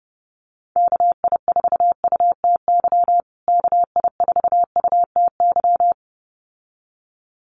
Morse code, heard as a keyed tone, about 30 wpm, 700 Hz. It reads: KI4UTY KI4UTY
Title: USA Call Signs Repeated 2x 30wpm